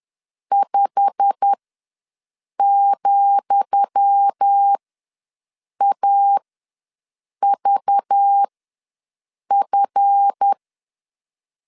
Morse code, heard as a keyed tone, about 10 wpm, spaced as for 6 wpm, 800 Hz. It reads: 5,AVF
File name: LEZIONI / 10 SIGNS / 10D COMA ws 10wpm 4314